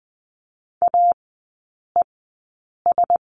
Morse code, heard as a keyed tone, about 20 wpm, spaced as for 8 wpm, 700 Hz. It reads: AES
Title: 053 Lesson 10e - Letter r - Three Characters 20wpm